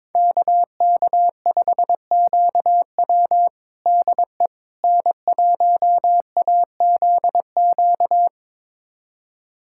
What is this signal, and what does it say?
Morse code, heard as a keyed tone, about 22 wpm, 700 Hz. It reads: KK5QW DE N1AZQ
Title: USA Callsign DE Callsign - 5 Characters - 22wpm